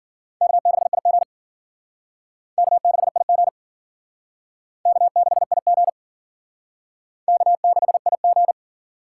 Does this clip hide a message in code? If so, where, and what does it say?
Yes: Morse code from 0.4 to 8.5 s, heard as a keyed tone, about 52 wpm, 700 Hz. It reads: K6IG K6IG K6IG K6IG